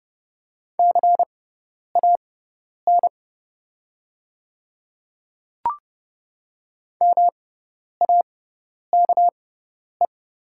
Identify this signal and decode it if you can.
Morse code, heard as a keyed tone, about 30 wpm, spaced as for 10 wpm, 700 Hz. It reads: CAN MAKE
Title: 6x Character Spacing - Top 100 Words - 30wpm